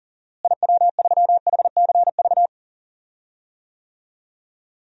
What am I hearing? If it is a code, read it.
Morse code, heard as a keyed tone, about 40 wpm, 700 Hz. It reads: IW3HCV
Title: DXCC-100 Call Signs - 6 Characters - Repeated 3x - 40wpm